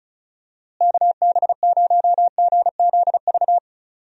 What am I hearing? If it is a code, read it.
Morse code, heard as a keyed tone, about 34 wpm, 700 Hz. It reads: KB0GZV
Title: USA Call Signs - 6 Characters - Repeated 3x - 35wpm